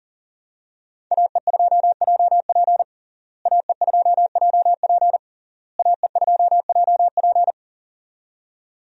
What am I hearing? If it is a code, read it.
Morse code, heard as a keyed tone, about 37 wpm, 700 Hz. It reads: AE2JP AE2JP AE2JP
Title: USA Call Signs - 5 Characters - Repeated 3x - 40wpm